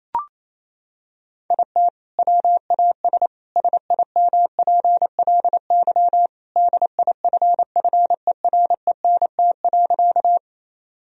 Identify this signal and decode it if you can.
Morse code, heard as a keyed tone, about 28 wpm, 700 Hz. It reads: IT WAS SIMPLY DIFFERENT.
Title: Sentences from Top 1000 Words - Review Part 4 - 28wpm